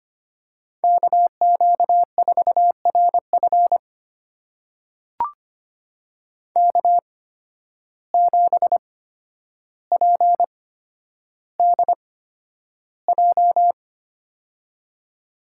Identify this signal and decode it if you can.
Morse code, heard as a keyed tone, about 25 wpm, spaced as for 11 wpm, 700 Hz. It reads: KQ4RF K7PDJ
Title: USA Call Signs - 5 Characters - 8x Character Spacing - 25wpm